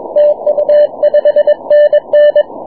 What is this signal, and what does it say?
Morse code, heard as a keyed tone, about 22 wpm, 590 Hz, with radio noise. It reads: TU 5NN
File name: wp / 9y4vu 6m cw